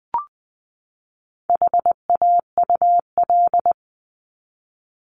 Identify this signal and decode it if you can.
Morse code, heard as a keyed tone, about 20 wpm, 700 Hz. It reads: HAUL
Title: 260 Lesson 42e - Words 20-20wpm